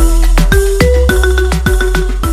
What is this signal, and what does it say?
There is no code.